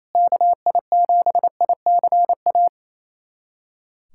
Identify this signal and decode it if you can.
Morse code, heard as a keyed tone, about 28 wpm, 700 Hz. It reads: KI7ICA
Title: USA Callsign DE Callsign - 6 Characters - 28wpm